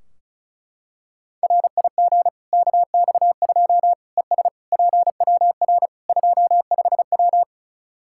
Morse code, heard as a keyed tone, about 36 wpm, 700 Hz. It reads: RIG KX2 ES PWR 25W
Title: Practice QSO - Exchange 2 - Info - 35wpm